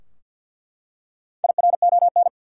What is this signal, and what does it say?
Morse code, heard as a keyed tone, about 48 wpm, 700 Hz. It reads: IRON
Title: Four Letter Words - 50wpm